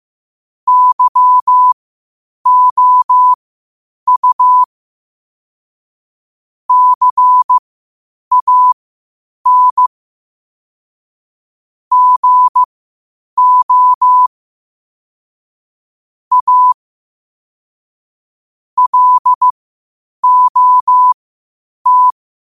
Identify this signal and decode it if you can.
Morse code, heard as a keyed tone, about 15 wpm, spaced as for 8 wpm, 1000 Hz. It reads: YOU CAN GO A LOT
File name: qotdfiles / 8wpm